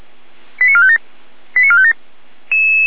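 Five-tone selective call (ZVEI), heard as quick runs of five tones, tones of about 70 ms. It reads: 89457 89457
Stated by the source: Five-tone ZVEI selective call received on an SDR.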